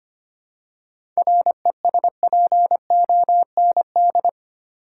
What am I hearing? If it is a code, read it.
Morse code, heard as a keyed tone, about 25 wpm, 700 Hz. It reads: RESPOND
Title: Seven Letter Words - 25wpm